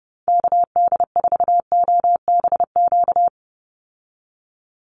Morse code, heard as a keyed tone, about 28 wpm, 700 Hz. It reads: KD4OBQ
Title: USA Call Signs 30wpm